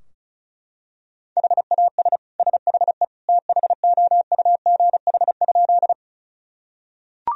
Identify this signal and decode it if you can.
Morse code, heard as a keyed tone, about 36 wpm, 700 Hz. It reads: HAS SHE THOUGH?
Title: Sentences from Top 1000 Words - Review Part 2 - 35wpm